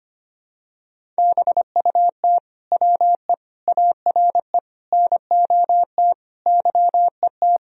Morse code, heard as a keyed tone, about 25 wpm, 700 Hz. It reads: BUT WE ARE NOT YET
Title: Sentences from Top 1000 Words - Review Part 1 - 25wpm